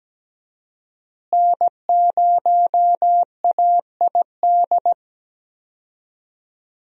Morse code, heard as a keyed tone, about 17 wpm, 700 Hz. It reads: N0AID
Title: USA Call Signs - 5 Characters - Repeated 2x - 17wpm